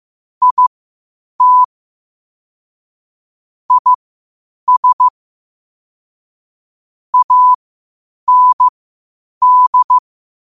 Morse code, heard as a keyed tone, about 15 wpm, spaced as for 8 wpm, 1000 Hz. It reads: IT IS AND
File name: qotdfiles / 8wpm